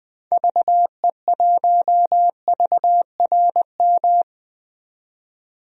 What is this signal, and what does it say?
Morse code, heard as a keyed tone, about 21 wpm, 700 Hz. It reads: VE1VRM
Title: DXCC-100 Call Signs - 6 Characters - 20wpm